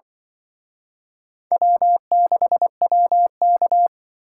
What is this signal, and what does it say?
Morse code, heard as a keyed tone, about 24 wpm, 700 Hz. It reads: W6WK
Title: Speed Racing - USA 1x2 Call Signs 20wpm